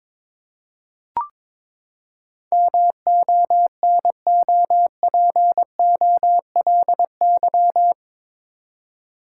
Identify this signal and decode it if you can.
Morse code, heard as a keyed tone, about 22 wpm, 700 Hz. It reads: MONOPOLY
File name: practice / 22wpm / Eight Letter Words - 22wpm